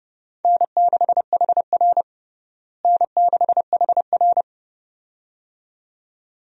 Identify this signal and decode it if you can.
Morse code, heard as a keyed tone, about 30 wpm, 700 Hz. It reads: N6HR N6HR
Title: USA Call Signs - 4 Characters - Repeated 2x - 30wpm